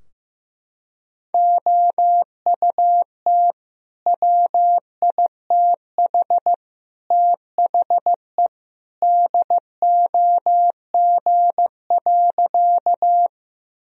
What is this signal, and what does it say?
Morse code, heard as a keyed tone, about 15 wpm, 700 Hz. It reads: OUT WITH THE DOG.
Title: Sentences from Top 1000 Words - Review Part 2 - 15wpm